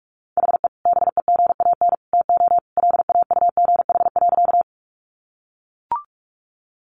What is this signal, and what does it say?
Morse code, heard as a keyed tone, about 44 wpm, 700 Hz. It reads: HE BEGAN TO LAUGH.
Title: Sentences from Top 500 Words 45wpm